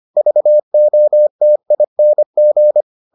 Morse code, heard as a keyed tone, about 25 wpm, 600 Hz. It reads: VOTING